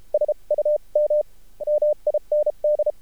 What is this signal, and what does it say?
Morse code, heard as a keyed tone, about 33 wpm, spaced as for 28 wpm, 600 Hz. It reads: SUM MIND
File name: archived / CRSnet